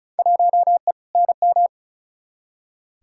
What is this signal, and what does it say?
Morse code, heard as a keyed tone, about 36 wpm, 700 Hz. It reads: 1E NM
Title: Field Day Exchanges - 35wpm